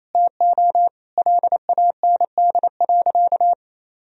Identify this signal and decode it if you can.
Morse code, heard as a keyed tone, about 28 wpm, 700 Hz. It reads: TO LAND.
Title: Sentences from Top 1000 Words - Review Part 1 - 28wpm